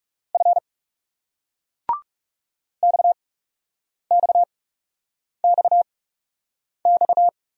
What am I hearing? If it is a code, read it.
Morse code, heard as a keyed tone, about 40 wpm, spaced as for 8 wpm, 700 Hz. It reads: F XXXX